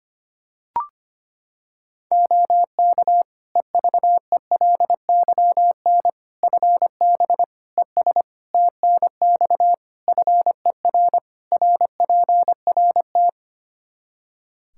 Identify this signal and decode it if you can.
Morse code, heard as a keyed tone, about 25 wpm, 700 Hz. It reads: OK EVELYN FB ES TNX FER RPRT
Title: Practice QSO - Exchange 2 - Info - 25wpm